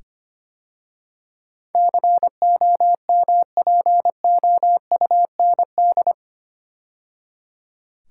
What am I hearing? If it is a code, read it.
Morse code, heard as a keyed tone, about 25 wpm, 700 Hz. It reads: COMPOUND
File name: 8 Letter Words - First 3 Letters Given - 25wpm